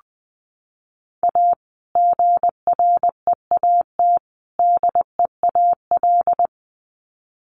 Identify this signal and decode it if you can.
Morse code, heard as a keyed tone, about 20 wpm, 700 Hz. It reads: A GREAT DEAL
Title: Sets of 3 Words 20wpm